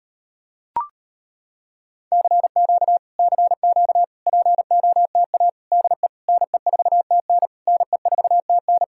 Morse code, heard as a keyed tone, about 33 wpm, 700 Hz. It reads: CQ CQ POTA DE NE4TN NE4TN